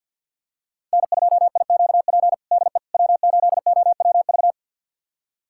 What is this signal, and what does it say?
Morse code, heard as a keyed tone, about 52 wpm, 700 Hz. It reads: N2IQP DE W8OWV
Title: USA Callsign DE Callsign - 5 Characters - 50wpm